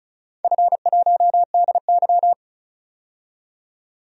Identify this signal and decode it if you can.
Morse code, heard as a keyed tone, about 36 wpm, 700 Hz. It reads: F1DY
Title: DXCC-100 Call Signs - 4 Characters - 35wpm